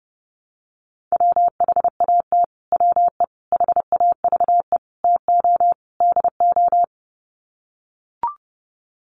Morse code, heard as a keyed tone, about 30 wpm, 700 Hz. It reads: WHAT WE HAVE TO DO
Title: Sets of 5 Words Encore 30wpm